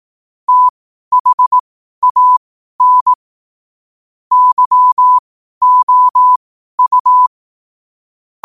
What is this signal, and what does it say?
Morse code, heard as a keyed tone, about 18 wpm, spaced as for 12 wpm, 1000 Hz. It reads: THAN YOU